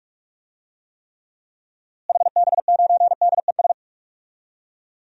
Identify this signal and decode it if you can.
Morse code, heard as a keyed tone, about 46 wpm, 700 Hz. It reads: HB9DES
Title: DXCC-100 Call Signs - 6 Characters - Repeated 2x - 45wpm